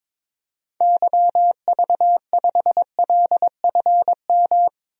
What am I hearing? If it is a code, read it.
Morse code, heard as a keyed tone, about 22 wpm, 700 Hz. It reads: YV5LFM
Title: DXCC-100 Call Signs - 6 Characters - 22wpm